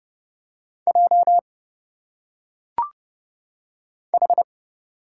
Morse code, heard as a keyed tone, about 29 wpm, 700 Hz. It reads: J H
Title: Single Letters - Phonetic Pronunciation 30wpm